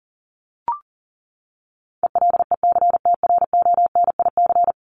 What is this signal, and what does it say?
Morse code, heard as a keyed tone, about 40 wpm, 700 Hz. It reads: ELECTRONIC
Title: Top 1001-1250 Words 40wpm